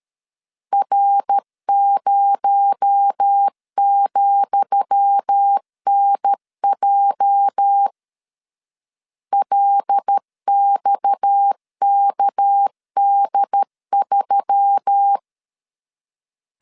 Morse code, heard as a keyed tone, about 13 wpm, 800 Hz. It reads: R0,NJ LXKD3
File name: LEZIONI / 10 SIGNS / 10D COMA 12wpm 2544